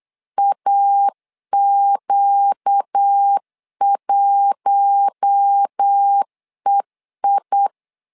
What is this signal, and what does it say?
Morse code, heard as a keyed tone, about 8 wpm, 800 Hz. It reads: AQ1EI
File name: LEZIONI / 4 Q1 8wpm 3035